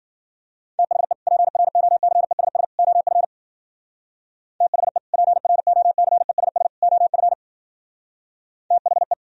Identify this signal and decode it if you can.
Morse code, heard as a keyed tone, about 50 wpm, 700 Hz. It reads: THE PROCESS OF THE PROCESS OF THE